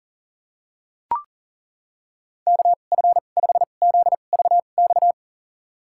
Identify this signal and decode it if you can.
Morse code, heard as a keyed tone, about 40 wpm, spaced as for 29 wpm, 700 Hz. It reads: KF5ZVX
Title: USA Call Signs - 6 Characters - 2x Character Spacing - 40wpm